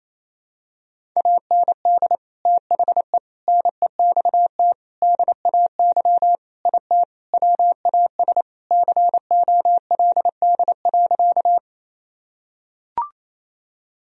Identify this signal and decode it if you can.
Morse code, heard as a keyed tone, about 28 wpm, 700 Hz. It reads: AND THE NEXT DAY IT WAS COLD.
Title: Sentences from Top 1000 Words - Review Part 1 - 28wpm